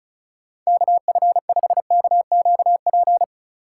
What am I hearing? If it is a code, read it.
Morse code, heard as a keyed tone, about 34 wpm, 700 Hz. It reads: KF5KQP
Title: USA Call Signs - 6 Characters - 2x Character Spacing - 35wpm